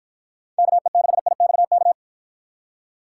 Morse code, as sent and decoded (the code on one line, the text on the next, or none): -.- . -.... .. -..- -.-
KE6IXK